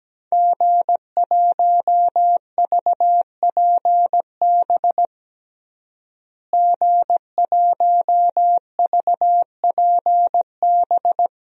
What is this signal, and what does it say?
Morse code, heard as a keyed tone, about 17 wpm, 700 Hz. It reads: G1VPB G1VPB
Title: DXCC-100 Call Signs - 5 Characters - Repeated 3x - 17wpm